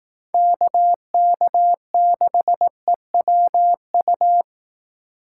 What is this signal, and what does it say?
Morse code, heard as a keyed tone, about 18 wpm, 700 Hz. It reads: KK6EWU
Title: Speed Racing - USA 2x3 Call Signs 15wpm